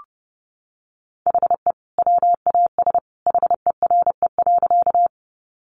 Morse code, heard as a keyed tone, about 30 wpm, 700 Hz. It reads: HE WAS HERE.
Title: Sentences from Top 200 Words 30wpm